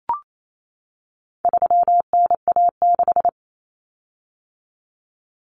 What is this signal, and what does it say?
Morse code, heard as a keyed tone, about 28 wpm, 700 Hz. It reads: 3NA6
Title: Common Cut Numbers - 4-Digit 28wpm